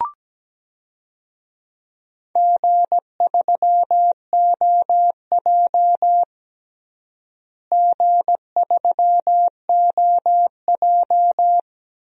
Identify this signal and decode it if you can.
Morse code, heard as a keyed tone, about 17 wpm, 700 Hz. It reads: G3OJ G3OJ
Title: DXCC-100 Call Signs - 4 Characters - Repeated 2x - 17wpm